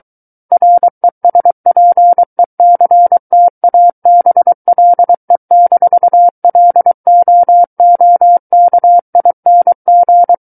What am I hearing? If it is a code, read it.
Morse code, heard as a keyed tone, about 23 wpm, 700 Hz. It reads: RESPECTABLE-LOOKING
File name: Book One / 14-Chapter 14 - 23 WPM